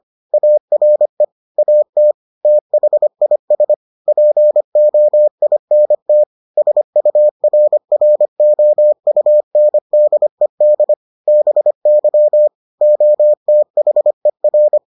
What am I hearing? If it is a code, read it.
Morse code, heard as a keyed tone, about 25 wpm, 600 Hz. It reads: ARE AT THIS POINT SURROUNDED BY OTHER